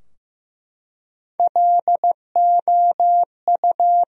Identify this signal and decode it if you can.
Morse code, heard as a keyed tone, about 15 wpm, 700 Hz. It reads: LOU
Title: USA Common Names - Max 3 Letters - 15wpm